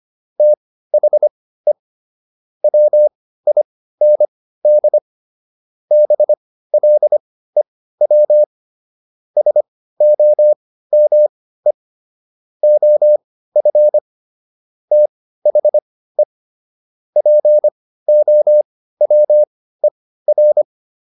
Morse code, heard as a keyed tone, about 25 wpm, spaced as for 15 wpm, 600 Hz. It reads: THE WIND BLEW SOME OF THE POWER